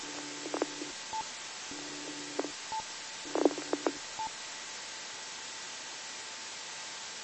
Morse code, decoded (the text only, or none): none